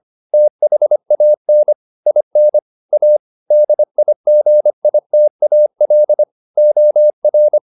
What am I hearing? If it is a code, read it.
Morse code, heard as a keyed tone, about 25 wpm, 600 Hz. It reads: THAN IN A DIGITAL OR